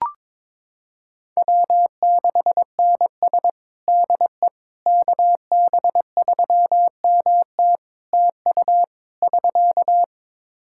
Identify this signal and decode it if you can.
Morse code, heard as a keyed tone, about 22 wpm, 700 Hz. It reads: W6NS DE KB3MT TU <SK>
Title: Practice QSO - Exchange 4 - Ending - 22wpm